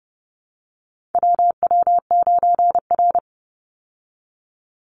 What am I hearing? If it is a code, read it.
Morse code, heard as a keyed tone, about 30 wpm, 700 Hz. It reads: WW9R
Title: Call Signs Repeated 3x - 30wpm